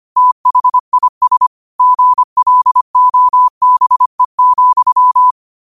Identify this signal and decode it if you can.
Morse code, heard as a keyed tone, about 25 wpm, 1000 Hz. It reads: THIS GLOBE,
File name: qotdfiles / 25wpm